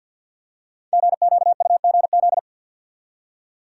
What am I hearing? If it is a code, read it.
Morse code, heard as a keyed tone, about 48 wpm, 700 Hz. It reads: GQUGZ
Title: Code Groups - 5 Letters - 50wpm